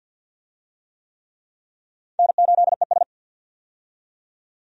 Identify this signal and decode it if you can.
Morse code, heard as a keyed tone, about 48 wpm, 700 Hz. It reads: N8ES
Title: USA Call Signs - 4 Characters - 50wpm